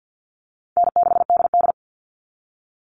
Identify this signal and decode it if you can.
Morse code, heard as a keyed tone, about 48 wpm, 700 Hz. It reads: N6DD
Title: All Cut Numbers - 4-Digit 50wpm